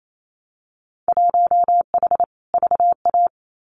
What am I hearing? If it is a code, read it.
Morse code, heard as a keyed tone, about 28 wpm, 700 Hz. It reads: 1H VA
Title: Winter Field Day Exchanges 28wpm